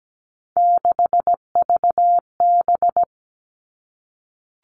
Morse code, heard as a keyed tone, about 17 wpm, 700 Hz. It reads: 6VB
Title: Three Challenging Letter-Number Combinations 17wpm